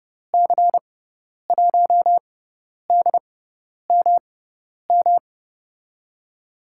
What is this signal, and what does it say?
Morse code, heard as a keyed tone, about 30 wpm, spaced as for 10 wpm, 700 Hz. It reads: C1DMM
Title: USA Call Signs - 6x Character Spacing - 30wpm